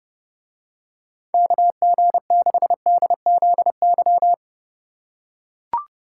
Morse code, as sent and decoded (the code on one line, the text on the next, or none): -.- --. -.... -.. --.. -.--
KG6DZY